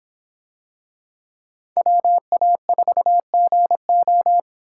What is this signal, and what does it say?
Morse code, heard as a keyed tone, about 25 wpm, 700 Hz. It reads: WA4GO